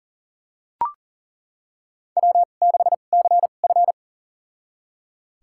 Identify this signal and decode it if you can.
Morse code, heard as a keyed tone, about 40 wpm, spaced as for 29 wpm, 700 Hz. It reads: W6CF
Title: USA Call Signs - 4 Characters - 2x Character Spacing - 40wpm